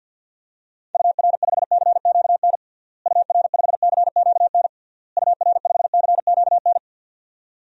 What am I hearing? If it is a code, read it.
Morse code, heard as a keyed tone, about 51 wpm, 700 Hz. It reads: UR5CQN UR5CQN UR5CQN
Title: DXCC-100 Call Signs - 6 Characters - Repeated 3x - 50wpm